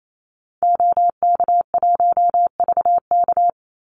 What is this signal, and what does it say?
Morse code, heard as a keyed tone, about 28 wpm, 700 Hz. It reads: OK1VK
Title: Call Signs - 28wpm